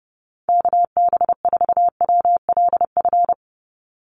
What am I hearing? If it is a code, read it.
Morse code, heard as a keyed tone, about 30 wpm, 700 Hz. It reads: KB4WLF
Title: Call Signs - 30wpm